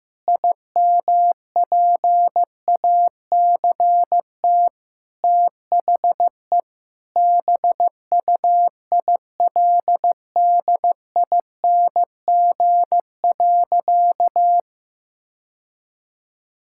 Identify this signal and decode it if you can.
Morse code, heard as a keyed tone, about 16 wpm, 700 Hz. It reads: IMPACT THE BUILDING.
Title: Sentences from Top 1000 Words - Review Part 6 - 15wpm